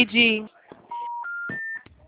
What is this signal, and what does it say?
Special information tone, with speech.
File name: out- -2019